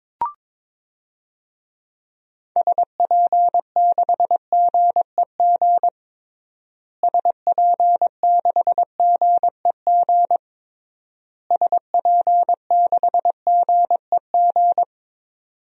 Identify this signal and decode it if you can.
Morse code, heard as a keyed tone, about 21 wpm, 700 Hz. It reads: SP6GEG SP6GEG SP6GEG
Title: DXCC-100 Call Signs - 6 Characters - Repeated 3x - 22wpm